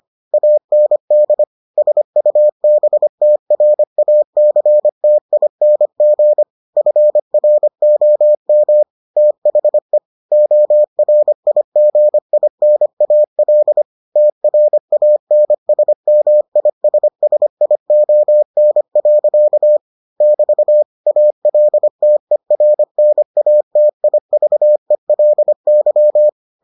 Morse code, heard as a keyed tone, about 25 wpm, 600 Hz. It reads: AND SUBTRACTING FROM THE ORIGINAL TRANSMISSION. <BT> ALTERNATIVELY